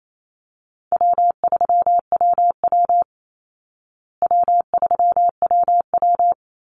Morse code, heard as a keyed tone, about 28 wpm, 700 Hz. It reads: W3WW W3WW
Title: Call Signs Repeated 3x - 28wpm